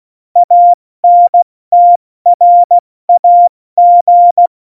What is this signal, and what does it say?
Morse code, heard as a keyed tone, about 16 wpm, spaced as for 14 wpm, 700 Hz. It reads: ANTRAG
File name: Morsekurs / Witze / Witz 04. Juni 2019, 70 F80